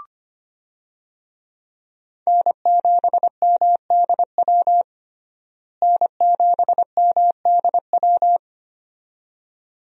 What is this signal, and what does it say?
Morse code, heard as a keyed tone, about 25 wpm, 700 Hz. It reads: N7MDW N7MDW
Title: USA Call Signs - 5 Characters - Repeated 2x - 25wpm